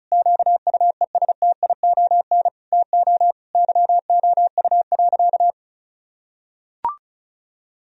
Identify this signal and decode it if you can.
Morse code, heard as a keyed tone, about 36 wpm, 700 Hz. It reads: QUESTION TO YOU.
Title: Sentences from Top 1000 Words - Review Part 2 - 35wpm